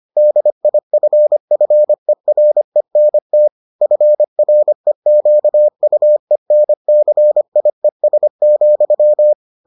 Morse code, heard as a keyed tone, about 25 wpm, 600 Hz. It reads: DIFFERENT FREQUENCIES,